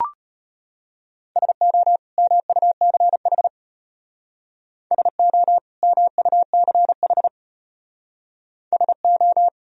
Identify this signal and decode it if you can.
Morse code, heard as a keyed tone, about 33 wpm, 700 Hz. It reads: SO MUCH SO MUCH SO